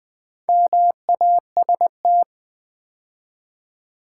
Morse code, heard as a keyed tone, about 20 wpm, 700 Hz. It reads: MAST
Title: Build A Word - 2 Letters At A Time - 6-Letter Words - 20wpm